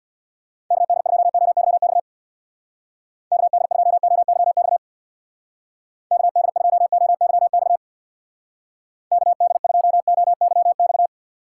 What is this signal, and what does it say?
Morse code, heard as a keyed tone, about 60 wpm, 700 Hz. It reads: KD2OYX KD2OYX KD2OYX KD2OYX
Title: Speed Racing - USA Call Signs 50wpm